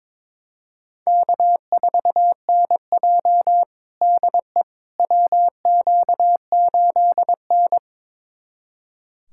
Morse code, heard as a keyed tone, about 22 wpm, 700 Hz. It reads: K4NJ DE WQ8N
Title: USA Callsign DE Callsign - 4 Characters - 22wpm